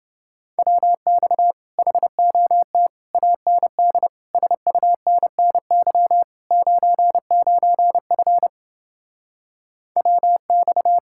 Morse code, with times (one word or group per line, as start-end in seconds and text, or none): 0.6-1.5 WX
1.8-2.9 HOT
3.1-4.1 AND
4.3-6.2 SUNNY
6.5-8.5 99F
10.0-11.0 WX